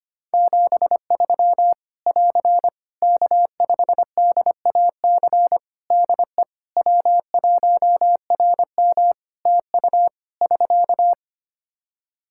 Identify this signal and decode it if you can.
Morse code, heard as a keyed tone, about 25 wpm, 700 Hz. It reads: 73 <AR> K5DAC DE W1RM TU <SK>
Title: Practice QSO - Exchange 4 - Ending - 25wpm